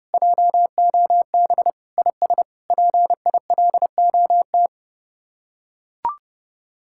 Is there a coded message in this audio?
Yes: Morse code from 0.1 to 4.7 s, heard as a keyed tone, about 29 wpm, 700 Hz. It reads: JOB IS PILOT